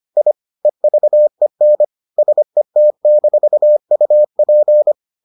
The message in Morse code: I EVEN SET-UP

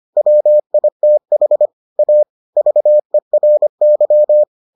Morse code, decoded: WITH A VERY

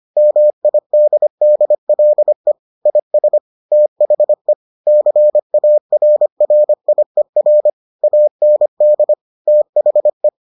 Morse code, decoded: MIDDLE IS THE CARRIER AND THE

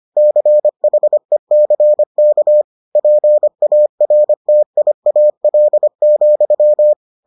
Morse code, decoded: CHECK PARTIAL,

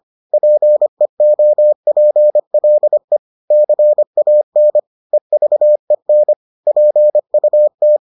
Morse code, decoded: PEOPLE CAN EVEN PUT

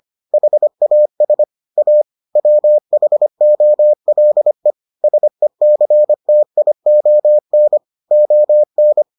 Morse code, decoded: HAS A WHOLE SECTION ON

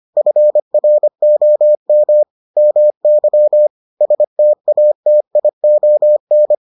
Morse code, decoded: FROM MY STATION

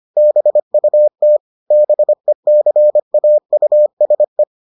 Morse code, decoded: BUT BECAUSE

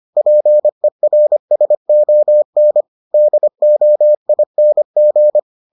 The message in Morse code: PERSON DOING

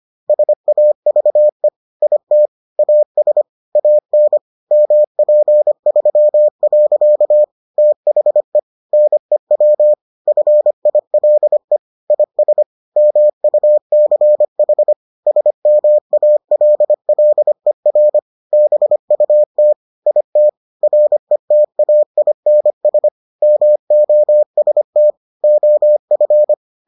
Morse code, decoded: SAVE IT AS AN MP3. THE NEW FILE IS MUCH SMALLER BUT IT RETAINS MOST OF

